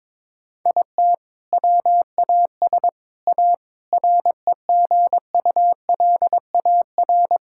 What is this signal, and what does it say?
Morse code, heard as a keyed tone, about 22 wpm, 700 Hz. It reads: IT WAS A REGULAR
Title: Sentences from Top 1000 Words - Review Part 6 - 22wpm